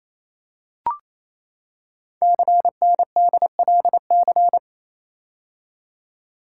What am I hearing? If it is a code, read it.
Morse code, heard as a keyed tone, about 27 wpm, 700 Hz. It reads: CNDLC
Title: Code Groups - 5 Letters - 28wpm